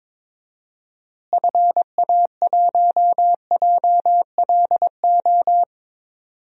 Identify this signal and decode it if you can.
Morse code, heard as a keyed tone, about 22 wpm, 700 Hz. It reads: FA1JLO
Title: DXCC-100 Call Signs - 6 Characters - Repeated 2x - 22wpm